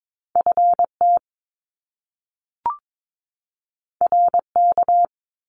Morse code, heard as a keyed tone, about 22 wpm, 700 Hz. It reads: FT RK
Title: Top Two Letter Combinations in Words 22wpm